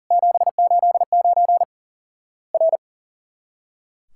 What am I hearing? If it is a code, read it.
Morse code, heard as a keyed tone, about 40 wpm, 700 Hz. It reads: 789 R
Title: Worked All Europe DX Contest - QTC Exchange - 40wpm